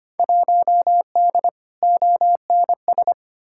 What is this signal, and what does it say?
Morse code, heard as a keyed tone, about 25 wpm, 700 Hz. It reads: 1D ONS
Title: Field Day Exchanges - 25wpm